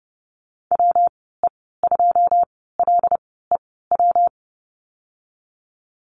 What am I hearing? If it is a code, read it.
Morse code, heard as a keyed tone, about 30 wpm, 700 Hz. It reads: W E 2 L E W
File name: Lesson 19g - Letter w - Call Signs 30wpm